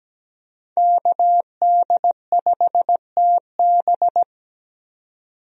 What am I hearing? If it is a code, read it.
Morse code, heard as a keyed tone, about 17 wpm, 700 Hz. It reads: KD5TB
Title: USA Call Signs - 5 Characters - 17wpm